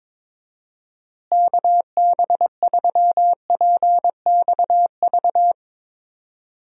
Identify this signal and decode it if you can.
Morse code, heard as a keyed tone, about 22 wpm, 700 Hz. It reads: KB3PXV
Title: USA Call Signs - 6 Characters - Repeated 3x - 22wpm